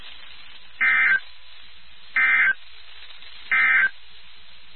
Emergency Alert System (SAME) tones, heard as harsh buzzing bursts. It is heard from 0.7 to 4.0 s.
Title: NOAA Weather Radio WXK79